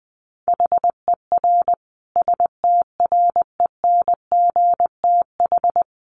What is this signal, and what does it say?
Morse code, heard as a keyed tone, about 20 wpm, 700 Hz. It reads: HER STRENGTH
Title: Sentences from Top 1300 Words 20wpm